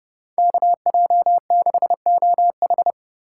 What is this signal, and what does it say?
Morse code, heard as a keyed tone, about 39 wpm, 700 Hz. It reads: KJ6OH